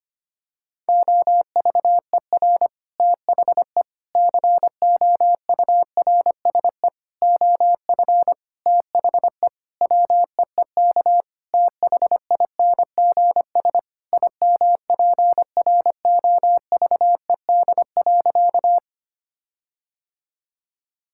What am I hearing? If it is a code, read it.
Morse code, heard as a keyed tone, about 25 wpm, 700 Hz. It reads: OVER THE COURSE OF THE WEEK THINGS IMPROVED.
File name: practice / 25wpm / Sentences with 4-Word Phrases - 25wpm